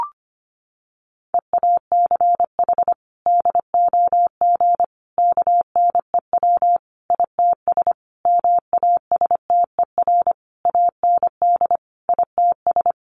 Morse code, heard as a keyed tone, about 25 wpm, 700 Hz. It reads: EACH DOG KNEW ITS MASTER AND ITS
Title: Sentences from Top 1300 Words 25wpm